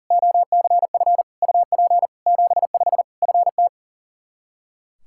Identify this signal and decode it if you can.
Morse code, heard as a keyed tone, about 40 wpm, 700 Hz. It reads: OCF UP 75 FT
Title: Practice QSO - Exchange 2 - Info - 40wpm